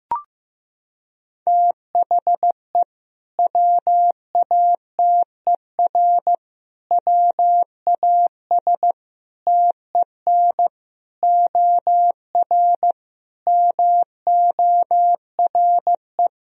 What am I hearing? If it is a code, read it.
Morse code, heard as a keyed tone, about 15 wpm, 700 Hz. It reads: THE WATER WAS TEN OR MORE
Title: Sentences from Top 1000 Words - Review Part 3 - 15wpm